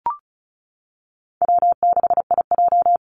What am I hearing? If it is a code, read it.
Morse code, heard as a keyed tone, about 34 wpm, 700 Hz. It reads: W6IJ
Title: Call Signs - 35wpm